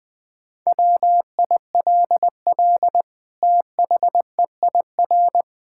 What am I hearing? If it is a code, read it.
Morse code, heard as a keyed tone, about 20 wpm, 700 Hz. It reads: WILL THEIR
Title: Sentences from Top 1000 Words - Review Part 5 - 20wpm